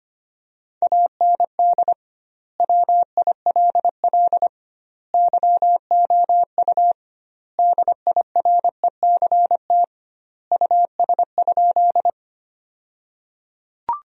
Morse code, heard as a keyed tone, about 25 wpm, 700 Hz. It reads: AND WILL YOU DIRECT US?
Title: 2x Word-Spacing - Sentences from Top 400 Words - 25wpm